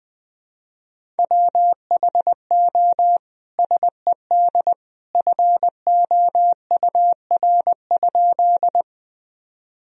Morse code, heard as a keyed tone, about 20 wpm, 700 Hz. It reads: WHO SED FOUR?
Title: Sentences from Top 300 Words and Abbreviations 20wpm